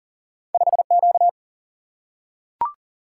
Morse code, heard as a keyed tone, about 42 wpm, 700 Hz. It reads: 5Q
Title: Single-Letter Single-Number 40wpm